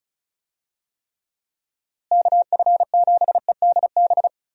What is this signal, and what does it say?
Morse code, heard as a keyed tone, about 34 wpm, 700 Hz. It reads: KF7EDB
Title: USA Call Signs - 6 Characters - Repeated 2x - 35wpm